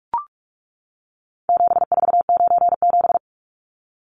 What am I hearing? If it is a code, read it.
Morse code, heard as a keyed tone, about 44 wpm, 700 Hz. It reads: 7497
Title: Numbers - 4-Digit 45wpm